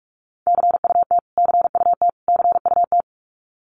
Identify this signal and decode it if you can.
Morse code, heard as a keyed tone, about 44 wpm, 700 Hz. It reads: CUT CUT CUT
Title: Top 500 Words Repeated 45wpm